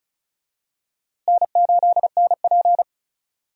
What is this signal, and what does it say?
Morse code, heard as a keyed tone, about 34 wpm, 700 Hz. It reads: N8NP
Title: USA Call Signs - 4 Characters - Repeated 3x - 35wpm